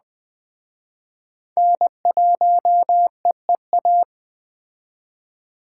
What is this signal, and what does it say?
Morse code, heard as a keyed tone, about 19 wpm, 700 Hz. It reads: N1EEA